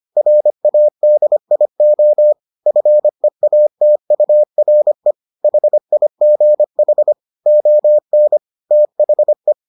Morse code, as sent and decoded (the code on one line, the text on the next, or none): .-. .- -.. .. --- / ..-. .- - ..- .-. / .... .. --. .... / --- -. / - .... .
RADIO FEATURE HIGH ON THE